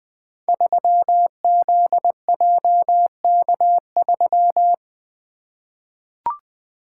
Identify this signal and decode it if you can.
Morse code, heard as a keyed tone, about 20 wpm, 700 Hz. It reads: 3ZJK3
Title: Code Groups - 5 Letter:Numbers - 20wpm